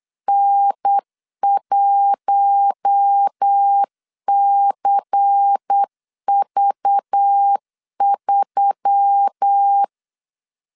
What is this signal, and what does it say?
Morse code, heard as a keyed tone, about 8 wpm, 800 Hz. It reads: N1CV3
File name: LEZIONI / 8 YJ45 / 8 YJ4 8wpm 3246